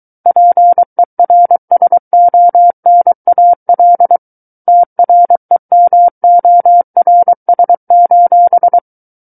Morse code, heard as a keyed tone, about 23 wpm, 700 Hz. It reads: PERSONAL TREMORS:
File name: Book One / 14-Chapter 14 - 23 WPM